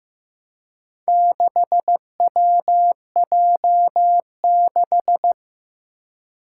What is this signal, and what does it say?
Morse code, heard as a keyed tone, about 15 wpm, 700 Hz. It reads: 6WJ6
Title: Four Challenging Letter-Number Combinations 15wpm